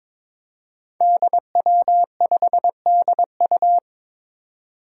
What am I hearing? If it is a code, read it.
Morse code, heard as a keyed tone, about 22 wpm, 700 Hz. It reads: DW5DU
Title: Five Challenging Letter-Number Combinations 22wpm